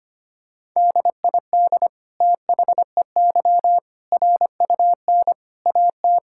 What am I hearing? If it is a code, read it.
Morse code, heard as a keyed tone, about 25 wpm, 700 Hz. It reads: DID THEY RUN AT
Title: Sentences from Top 1000 Words - Review Part 5 - 25wpm